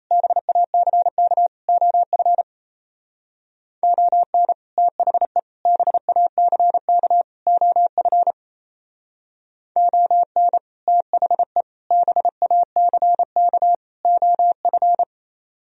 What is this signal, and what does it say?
Morse code, heard as a keyed tone, about 35 wpm, 700 Hz. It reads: BACK OF ON THE BACK OF ON THE BACK OF